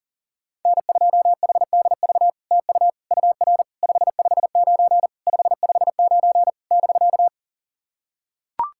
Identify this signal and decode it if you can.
Morse code, heard as a keyed tone, about 40 wpm, 700 Hz. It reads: N2HDV TU UR 559 559 <BK>
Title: SOTA QSO Practice 40wpm